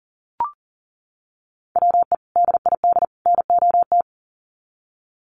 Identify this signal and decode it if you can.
Morse code, heard as a keyed tone, about 40 wpm, 700 Hz. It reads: WE DID NOT
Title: Sets of 3 Words 40wpm